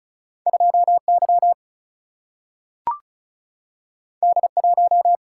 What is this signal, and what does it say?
Morse code, heard as a keyed tone, about 36 wpm, 700 Hz. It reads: 2 Y D 1
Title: Single-Letter Single-Number 35wpm